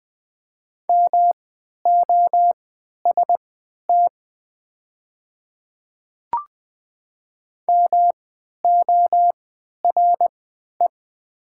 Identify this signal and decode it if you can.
Morse code, heard as a keyed tone, about 20 wpm, spaced as for 11 wpm, 700 Hz. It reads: MOST MORE
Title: 3x Character Spacing - Top 100 Words - 20wpm